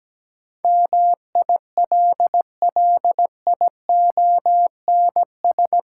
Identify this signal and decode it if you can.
Morse code, heard as a keyed tone, about 17 wpm, 700 Hz. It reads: MILLIONS
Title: Sets of 5 Words - New Word Drill - 17wpm